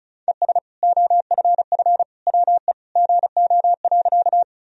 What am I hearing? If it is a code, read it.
Morse code, heard as a keyed tone, about 34 wpm, 700 Hz. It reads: ES OFF WE GO.
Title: Sentences from Top 200 Words and Abbreviations 35wpm